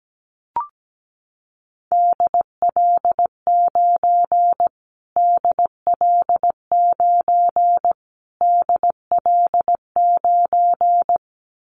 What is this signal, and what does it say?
Morse code, heard as a keyed tone, about 17 wpm, 700 Hz. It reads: DL9 DL9 DL9
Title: Callsign Prefixes for Top 25 DXCC - Repeated 3x - 17wpm